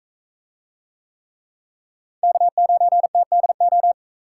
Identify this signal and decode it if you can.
Morse code, heard as a keyed tone, about 41 wpm, 700 Hz. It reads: K9TDO